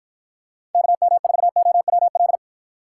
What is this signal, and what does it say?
Morse code, heard as a keyed tone, about 53 wpm, 700 Hz. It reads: KM4OWL